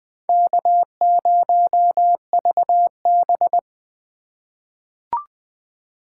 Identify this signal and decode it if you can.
Morse code, heard as a keyed tone, about 20 wpm, 700 Hz. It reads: K0VB
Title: USA Call Signs - 2x Character Spacing - 20wpm